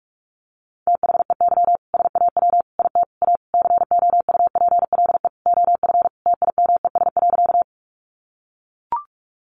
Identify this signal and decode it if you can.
Morse code, heard as a keyed tone, about 44 wpm, 700 Hz. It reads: THEY SAW IT A COUPLE OF TIMES.
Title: Sentences from Top 1100 Words 45wpm